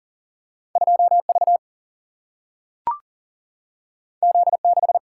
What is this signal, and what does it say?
Morse code, heard as a keyed tone, about 40 wpm, 700 Hz. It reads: 2V Z6
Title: Single-Letter Single-Number 40wpm